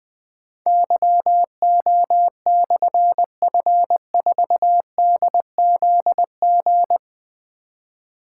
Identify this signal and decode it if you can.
Morse code, heard as a keyed tone, about 20 wpm, 700 Hz. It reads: YO/F4DZG
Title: Difficult Call Signs 20wpm